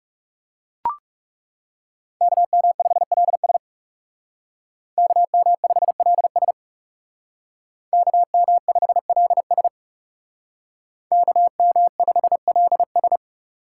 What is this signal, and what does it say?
Morse code, heard as a keyed tone, about 36 wpm, 700 Hz. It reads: KM5LS KM5LS KM5LS KM5LS
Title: Speed Racing - USA 2x2 Call Signs 30wpm